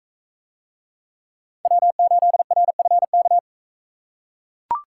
Morse code, heard as a keyed tone, about 42 wpm, 700 Hz. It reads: W8RFK